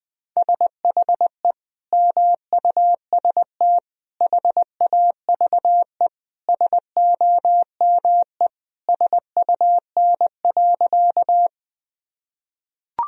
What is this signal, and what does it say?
Morse code, heard as a keyed tone, about 20 wpm, 700 Hz. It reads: SHE MUST HAVE SOME SUN.
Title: Sentences from Top 1000 Words - Review Part 1 - 20wpm